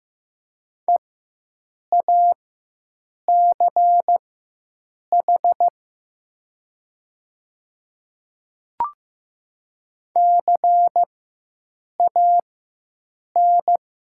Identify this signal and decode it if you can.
Morse code, heard as a keyed tone, about 15 wpm, spaced as for 7 wpm, 700 Hz. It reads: EACH CAN
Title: 4x Character Spacing - Top 100 Words - 15wpm